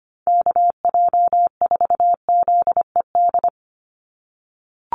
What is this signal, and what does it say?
Morse code, heard as a keyed tone, about 25 wpm, 700 Hz. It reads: KJ4ZED
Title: USA Call Signs 25wpm